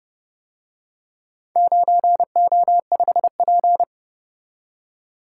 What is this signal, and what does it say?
Morse code, heard as a keyed tone, about 30 wpm, 700 Hz. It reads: 9O5P